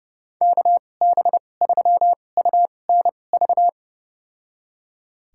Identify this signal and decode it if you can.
Morse code, heard as a keyed tone, about 31 wpm, spaced as for 22 wpm, 700 Hz. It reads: KB3UNV
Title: USA Call Signs - 6 Characters - 2x Character Spacing - 30wpm